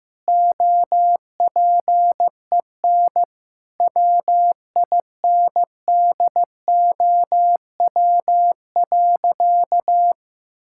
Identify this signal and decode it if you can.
Morse code, heard as a keyed tone, about 15 wpm, 700 Hz. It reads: OPEN WINDOW.
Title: Sentences from Top 1600 Words 15wpm